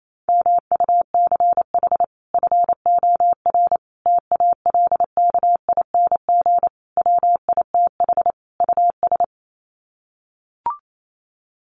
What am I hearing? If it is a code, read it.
Morse code, heard as a keyed tone, about 28 wpm, 700 Hz. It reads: MUCH FOR TALKING WITH US
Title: Sets of 5 Words Encore 28wpm